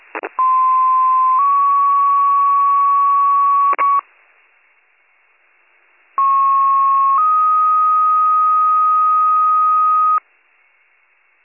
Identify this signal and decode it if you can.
Two-tone page, heard as a one-second tone, then a three-second tone, about 1040 then 1120 Hz, about 1090 then 1250 Hz. It is heard from 0.4 to 10.2 s.